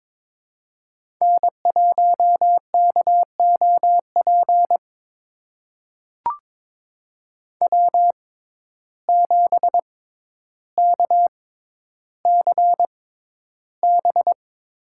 Morse code, heard as a keyed tone, about 22 wpm, spaced as for 11 wpm, 700 Hz. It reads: N1KOP W7KCB
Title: USA Call Signs - 5 Characters - 6x Character Spacing - 22wpm